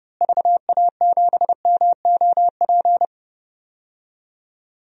Morse code, heard as a keyed tone, about 32 wpm, 700 Hz. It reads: VA7MOP
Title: DXCC-100 Call Signs - 6 Characters - Repeated 3x - 30wpm